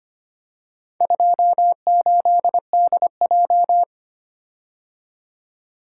Morse code, heard as a keyed tone, about 25 wpm, 700 Hz. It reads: 28DJ
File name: practice / 25wpm / Four Letter-Number Combinations - 25wpm